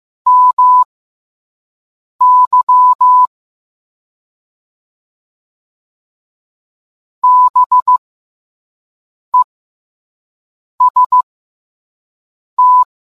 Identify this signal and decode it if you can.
Morse code, heard as a keyed tone, about 15 wpm, spaced as for 5 wpm, 1000 Hz. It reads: MY BEST